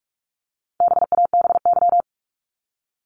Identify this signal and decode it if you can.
Morse code, heard as a keyed tone, about 44 wpm, 700 Hz. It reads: BABY
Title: Top 1001-1250 Words 45wpm